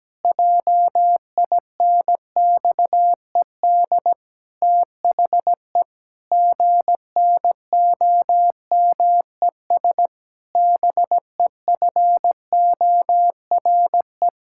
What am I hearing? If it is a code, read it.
Morse code, heard as a keyed tone, about 17 wpm, 700 Hz. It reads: JINXED THE GNOMES BEFORE
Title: Pangram Challenge v2 - 17wpm